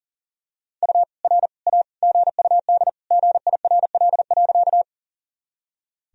Morse code, heard as a keyed tone, about 40 wpm, 700 Hz. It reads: U R A GUD GIRL.
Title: Sentences from Top 400 Words and Abbreviations 40wpm